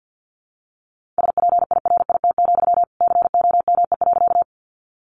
Morse code, heard as a keyed tone, about 50 wpm, 700 Hz. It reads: SPIRIT, COME.
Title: Sentences from Top 1400 Words 50wpm